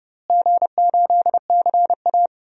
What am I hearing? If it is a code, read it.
Morse code, heard as a keyed tone, about 30 wpm, 700 Hz. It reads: G8CA
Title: DXCC-100 Call Signs - 4 Characters - Repeated 2x - 30wpm